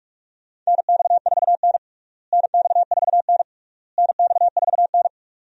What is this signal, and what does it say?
Morse code, heard as a keyed tone, about 46 wpm, 700 Hz. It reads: NX4N NX4N NX4N
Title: USA Call Signs - 4 Characters - Repeated 3x - 45wpm